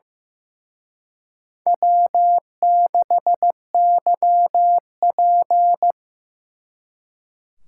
Morse code, heard as a keyed tone, about 18 wpm, 700 Hz. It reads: W6YP
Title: Speed Racing - USA Call Signs 15wpm